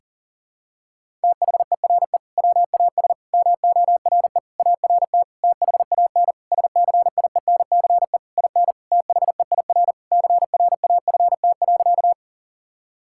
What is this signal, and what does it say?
Morse code, heard as a keyed tone, about 37 wpm, 700 Hz. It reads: THERE WAS MORE ART THAN SCIENCE IN THEIR CRAFT.
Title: Sentences from Top 1000 Words - Review Part 6 - 40wpm